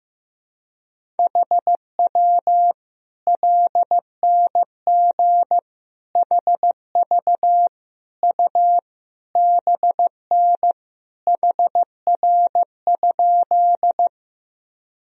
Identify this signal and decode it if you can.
Morse code, heard as a keyed tone, about 15 wpm, 700 Hz. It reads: HW LNG HV U BN HR?
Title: Sentences from Top 200 Words and Abbreviations 15wpm